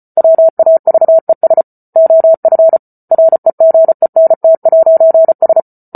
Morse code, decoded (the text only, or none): WAVES OF REGENT'S